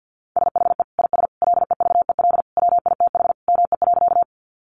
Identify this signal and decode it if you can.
Morse code, heard as a keyed tone, about 50 wpm, 700 Hz. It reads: SHE IS LEVEL WITH ME.
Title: Sentences from Top 700 Words 50wpm